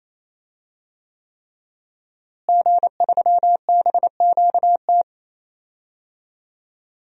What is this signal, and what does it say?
Morse code, heard as a keyed tone, about 28 wpm, 700 Hz. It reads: G3BQT
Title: DXCC-100 Call Signs - 5 Characters - Repeated 3x - 28wpm